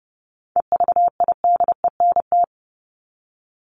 Morse code, heard as a keyed tone, about 30 wpm, 700 Hz. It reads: EVIDENT
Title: LY Suffix - Top 100 Words with Base Words 30wpm